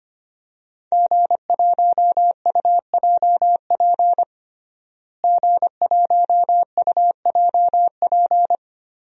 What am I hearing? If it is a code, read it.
Morse code, heard as a keyed tone, about 25 wpm, 700 Hz. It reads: G1UJP G1UJP
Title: DXCC-100 Call Signs - 5 Characters - Repeated 3x - 25wpm